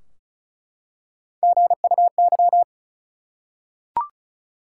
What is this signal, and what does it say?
Morse code, heard as a keyed tone, about 36 wpm, 700 Hz. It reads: GUY
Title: Three Letter Words - 35wpm